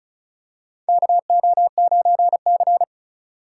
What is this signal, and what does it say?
Morse code, heard as a keyed tone, about 36 wpm, 700 Hz. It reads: KO9C
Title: USA Call Signs - 4x Character Spacing - 35wpm